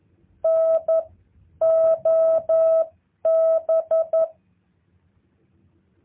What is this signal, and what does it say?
Morse code, heard as a keyed tone, about 11 wpm, spaced as for 9 wpm, 650 Hz. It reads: NOB
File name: jh8cbh / 2023cw